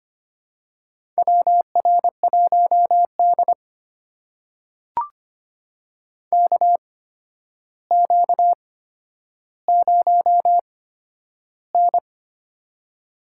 Morse code, heard as a keyed tone, about 25 wpm, spaced as for 11 wpm, 700 Hz. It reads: WR1D KQ0N
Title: USA Call Signs - 4 Characters - 8x Character Spacing - 25wpm